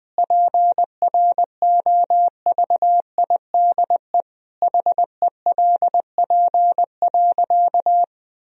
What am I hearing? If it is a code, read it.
Morse code, heard as a keyed tone, about 20 wpm, 700 Hz. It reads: PROVIDE HELP.
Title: Sentences from Top 1000 Words - Review Part 3 - 20wpm